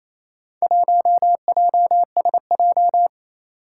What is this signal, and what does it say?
Morse code, heard as a keyed tone, about 29 wpm, 700 Hz. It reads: 1JSJ
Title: Four Challenging Letter-Number Combinations 28wpm